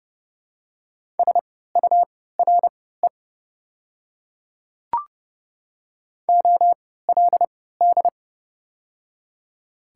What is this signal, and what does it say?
Morse code, heard as a keyed tone, about 30 wpm, spaced as for 17 wpm, 700 Hz. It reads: SURE OLD
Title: 3x Character Spacing - Top 300 Words - 30wpm